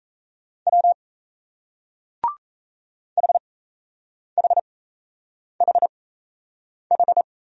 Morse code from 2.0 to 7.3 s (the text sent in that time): HHHH